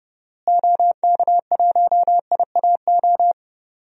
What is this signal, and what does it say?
Morse code, heard as a keyed tone, about 30 wpm, 700 Hz. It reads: OK1IAO
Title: DXCC-100 Call Signs - 6 Characters - Repeated 3x - 30wpm